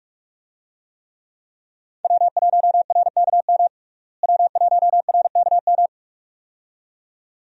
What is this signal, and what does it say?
Morse code, heard as a keyed tone, about 42 wpm, 700 Hz. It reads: W1RKM W1RKM
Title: USA Call Signs - 5 Characters - Repeated 2x - 45wpm